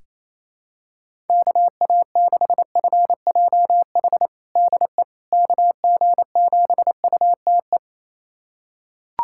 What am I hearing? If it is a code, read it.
Morse code, heard as a keyed tone, about 28 wpm, 700 Hz. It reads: KA6FJH DE KG7UTE
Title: USA Callsign DE Callsign - 6 Characters - 28wpm